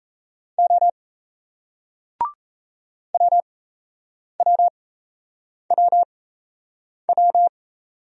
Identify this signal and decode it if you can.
Morse code, heard as a keyed tone, about 38 wpm, spaced as for 8 wpm, 700 Hz. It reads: O WWWW